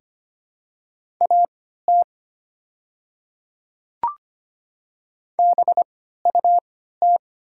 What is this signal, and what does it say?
Morse code, heard as a keyed tone, about 25 wpm, spaced as for 14 wpm, 690 Hz. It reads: AT BUT
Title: 3x Character Spacing - Top 100 Words - 25wpm